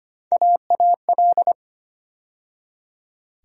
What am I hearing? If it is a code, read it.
Morse code, heard as a keyed tone, about 25 wpm, 700 Hz. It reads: AAL
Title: USA Call Sign Postfixes - 3 Letters 25wpm